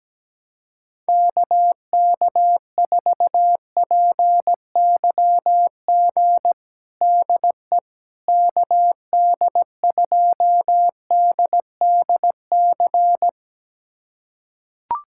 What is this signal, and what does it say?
Morse code, heard as a keyed tone, about 17 wpm, 700 Hz. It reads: KK4PYG DE KD2DDC
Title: USA Callsign DE Callsign 17wpm